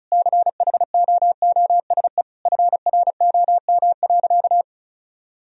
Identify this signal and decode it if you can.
Morse code, heard as a keyed tone, about 36 wpm, 700 Hz. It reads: CHOOSE FROM.
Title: Sentences from Top 1000 Words - Review Part 4 - 35wpm